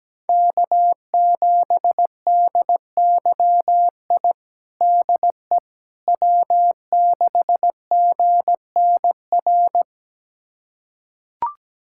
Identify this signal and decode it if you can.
Morse code, heard as a keyed tone, about 17 wpm, 700 Hz. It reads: K7DYI DE W6GNR
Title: USA Callsign DE Callsign - 5 Characters - 17wpm